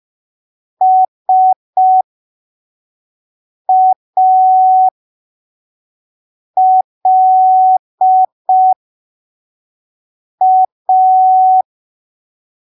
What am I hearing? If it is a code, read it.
Morse code, heard as a keyed tone, about 5 wpm, spaced as for 3 wpm, 750 Hz. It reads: SALA
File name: arquivos / texto / texto 26 05ppm